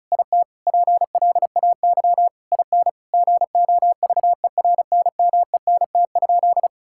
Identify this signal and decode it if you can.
Morse code, heard as a keyed tone, about 34 wpm, 700 Hz. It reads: IT PLAY IN GOVERNMENT?
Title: Sentences from Top 1000 Words - Review Part 6 - 35wpm